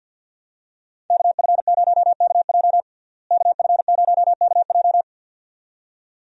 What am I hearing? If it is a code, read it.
Morse code, heard as a keyed tone, about 52 wpm, 700 Hz. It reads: KF0KJ KF0KJ
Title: USA Call Signs - 5 Characters - Repeated 2x - 50wpm